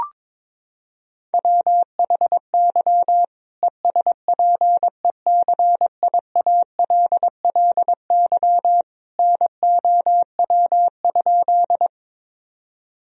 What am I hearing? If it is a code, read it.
Morse code, heard as a keyed tone, about 22 wpm, 700 Hz. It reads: WHY ESPECIALLY NOW?
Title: Sentences from Top 1000 Words - Review Part 4 - 22wpm